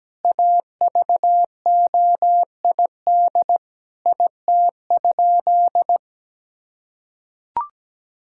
Morse code, heard as a keyed tone, about 17 wpm, 700 Hz. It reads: AVOID IT?
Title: Sentences from Top 1000 Words - Review Part 6 - 17wpm